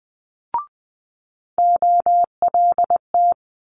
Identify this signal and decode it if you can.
Morse code, heard as a keyed tone, about 20 wpm, 700 Hz. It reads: OLT
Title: Multi-letter Phonemes Quick Fire 20wpm